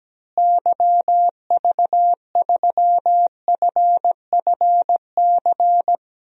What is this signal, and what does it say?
Morse code, heard as a keyed tone, about 17 wpm, 700 Hz. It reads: YV3FFC
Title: DXCC-100 Call Signs - 6 Characters - Repeated 3x - 17wpm